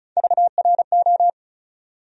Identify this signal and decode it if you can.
Morse code, heard as a keyed tone, about 36 wpm, 700 Hz. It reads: VRO
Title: USA Call Sign Postfixes - 3 Letters 35wpm